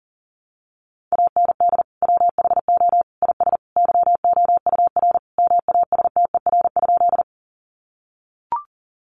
Morse code, heard as a keyed tone, about 40 wpm, 700 Hz. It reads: AND WHO IS YOUR MASTER?
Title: Sentences from Top 1300 Words 40wpm